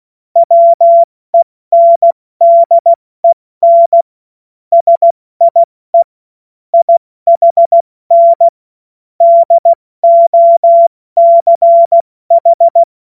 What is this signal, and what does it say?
Morse code, heard as a keyed tone, about 16 wpm, 680 Hz. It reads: WENDEN SIE IHN DOCH
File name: Morsekurs / Witze / Witz 04. Juli 2023 70 F80